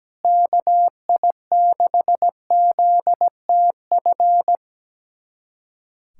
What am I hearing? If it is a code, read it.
Morse code, heard as a keyed tone, about 17 wpm, 700 Hz. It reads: KI6ZTF
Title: USA Callsign DE Callsign - 6 Characters - 17wpm